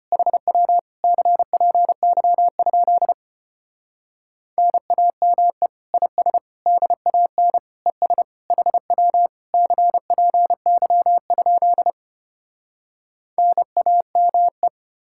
Morse code, heard as a keyed tone, about 33 wpm, 700 Hz. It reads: HW CPY? NAME IS DAN ES HW CPY? NAME